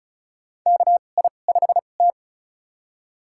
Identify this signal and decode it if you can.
Morse code, heard as a keyed tone, about 34 wpm, spaced as for 25 wpm, 700 Hz. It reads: KI5T
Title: USA Call Signs - 2x Character Spacing - 35wpm